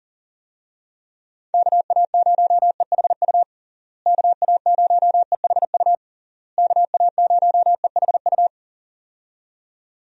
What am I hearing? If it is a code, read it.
Morse code, heard as a keyed tone, about 40 wpm, 700 Hz. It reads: KA0EHU KA0EHU KA0EHU
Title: USA Call Signs - 6 Characters - Repeated 3x - 40wpm